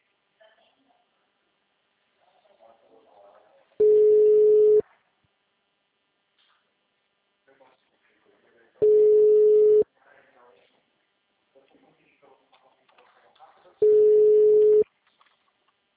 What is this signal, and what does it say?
Ringback tone (Europe, 425 Hz), with speech.